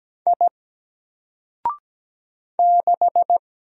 Morse code, heard as a keyed tone, about 17 wpm, 700 Hz. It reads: I 6
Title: Single Letter-Number 17wpm